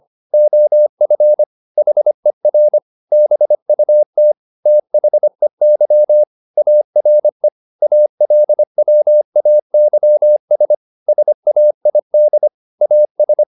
Morse code, heard as a keyed tone, about 25 wpm, 600 Hz. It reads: OF HER BUT THEY ARE ALWAYS SAID AS
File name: the old man and the sea part 1 25wpm